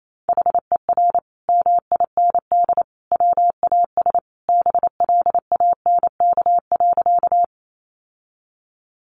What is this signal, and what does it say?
Morse code, heard as a keyed tone, about 28 wpm, 700 Hz. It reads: HER MIND WAS BLANK.
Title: Sentences from Top 1500 Words 28wpm